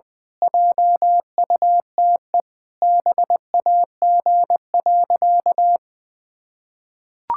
Morse code, heard as a keyed tone, about 21 wpm, 700 Hz. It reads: JUTE BAG.
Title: Pangram Challenge v2 - 20wpm